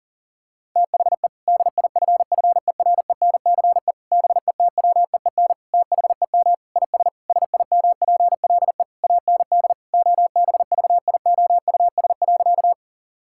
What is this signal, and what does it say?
Morse code, heard as a keyed tone, about 40 wpm, 700 Hz. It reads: THE DIFFERENCE BETWEEN THEM IS SIMPLE AND OBVIOUS.
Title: Sentences from Top 1600 Words 40wpm